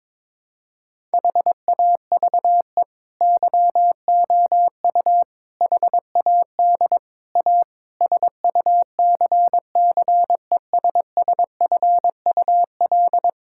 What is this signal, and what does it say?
Morse code, heard as a keyed tone, about 22 wpm, 700 Hz. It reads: HAVE YOU HAD A SUCCESSFUL
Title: Sentences from Top 1000 Words - Review Part 6 - 22wpm